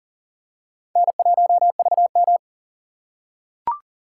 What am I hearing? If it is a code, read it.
Morse code, heard as a keyed tone, about 37 wpm, 700 Hz. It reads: N1VM
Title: USA Call Signs - 4 Characters - 3x Character Spacing - 40wpm